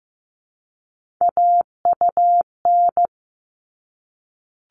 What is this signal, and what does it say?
Morse code, heard as a keyed tone, about 15 wpm, 700 Hz. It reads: AUN
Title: All Cut Numbers - 3-Digit 15wpm